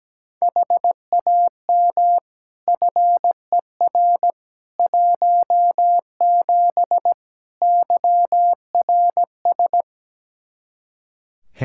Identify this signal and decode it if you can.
Morse code, heard as a keyed tone, about 17 wpm, 700 Hz. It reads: HAM FER 17 YRS
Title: Practice QSO - Exchange 3 - More Info - 17wpm